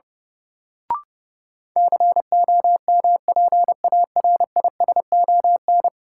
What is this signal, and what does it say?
Morse code, heard as a keyed tone, about 32 wpm, 700 Hz. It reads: COMPARISON
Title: Sentences from Top 1600 Words - New Word Drill - 30wpm